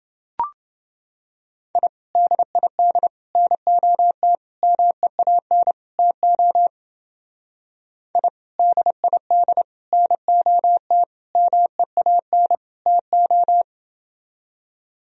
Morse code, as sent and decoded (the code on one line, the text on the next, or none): .. / -.. .. -.. / -. --- - / -- . .- -. / - --- / .. / -.. .. -.. / -. --- - / -- . .- -. / - ---
I DID NOT MEAN TO I DID NOT MEAN TO